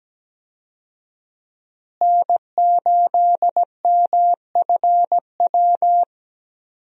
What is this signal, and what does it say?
Morse code, heard as a keyed tone, about 17 wpm, 700 Hz. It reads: N8MFW
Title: USA Call Signs - 5 Characters - Repeated 3x - 17wpm